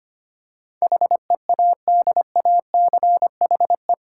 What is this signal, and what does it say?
Morse code, heard as a keyed tone, about 25 wpm, 700 Hz. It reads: HEADACHE
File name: Eight Letter Words - 25wpm